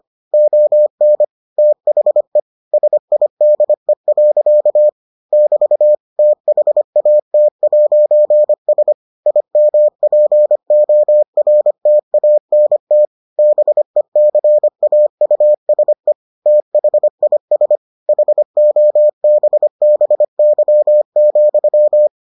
Morse code, heard as a keyed tone, about 25 wpm, 600 Hz. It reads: ON THE SIDE. <BT> THAT'S IMPORTANT BECAUSE THIS HOBBY,